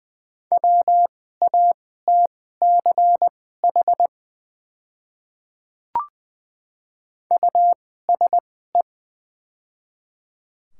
Morse code, heard as a keyed tone, about 20 wpm, spaced as for 14 wpm, 690 Hz. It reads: WATCH USE
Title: 2x Character Spacing - Top 500 Words - 20wpm